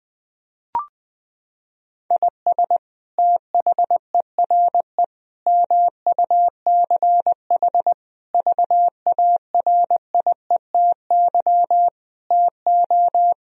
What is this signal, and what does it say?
Morse code, heard as a keyed tone, about 20 wpm, 700 Hz. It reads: IS THERE MUCH VARIETY TO
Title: Sentences from Top 1000 Words - Review Part 4 - 20wpm